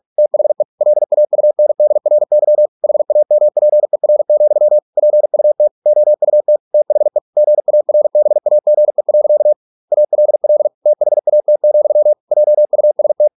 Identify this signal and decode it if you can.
Morse code, heard as a keyed tone, about 46 wpm, 600 Hz. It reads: THE LAUNDRY HAMPER, PUT OUT THE GARBAGE. ALL THAT, JUST